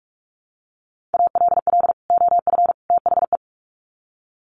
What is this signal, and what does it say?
Morse code, heard as a keyed tone, about 44 wpm, 700 Hz. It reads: ALL OF THE
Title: Sets of 3 Words 45wpm